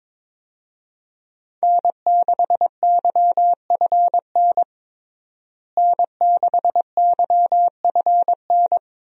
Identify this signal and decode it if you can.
Morse code, heard as a keyed tone, about 22 wpm, 700 Hz. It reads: N6YFN N6YFN
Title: USA Call Signs - 5 Characters - Repeated 3x - 22wpm